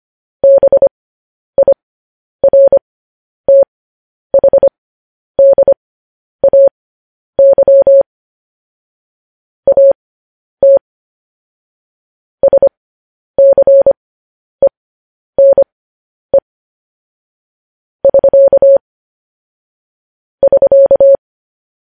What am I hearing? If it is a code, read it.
Morse code, heard as a keyed tone, about 25 wpm, spaced as for 10 wpm, 550 Hz. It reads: BIRTHDAY AT SCENE <SK> <SK>
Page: news headlines in morse code at 10 wpm